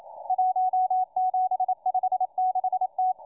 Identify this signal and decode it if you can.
Morse code, heard as a keyed tone, about 27 wpm, 740 Hz, with radio noise. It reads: 1756T